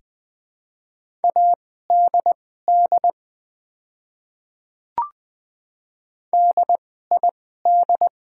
Morse code, heard as a keyed tone, about 20 wpm, spaced as for 14 wpm, 700 Hz. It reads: ADD DID
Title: 2x Character Spacing - Top 200 Words - 20wpm